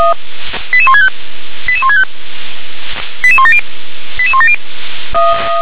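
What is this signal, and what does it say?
Five-tone selective call (ZVEI), heard as quick runs of five tones, tones of about 70 ms. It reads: 88165 88165 88170 88170
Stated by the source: Five-tone ZVEI selective call received on SDR.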